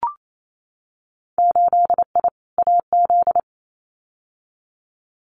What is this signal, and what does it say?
Morse code, heard as a keyed tone, about 28 wpm, 700 Hz. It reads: 8I AZ
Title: Winter Field Day Exchanges Encore 28wpm